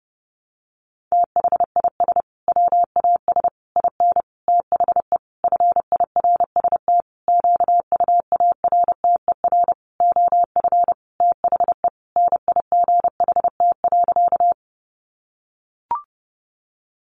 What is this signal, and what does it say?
Morse code, heard as a keyed tone, about 30 wpm, 700 Hz. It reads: THIS WAS IN THE FIRST QUARTER OF THE NIGHT.
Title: Sentences from Top 1300 Words 30wpm